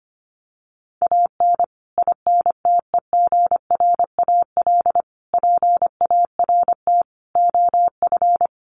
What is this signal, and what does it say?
Morse code, heard as a keyed tone, about 25 wpm, 700 Hz. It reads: AN INTEGRAL PART OF
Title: Sets of 5 Words 25wpm